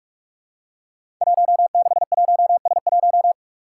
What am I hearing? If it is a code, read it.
Morse code, heard as a keyed tone, about 44 wpm, 700 Hz. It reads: 161S1
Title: Five Challenging Letter-Number Combinations 45wpm